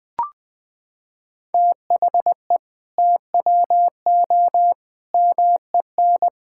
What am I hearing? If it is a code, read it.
Morse code, heard as a keyed tone, about 21 wpm, 700 Hz. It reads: THE TWO MEN
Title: Sentences from Top 1000 Words - Review Part 5 - 20wpm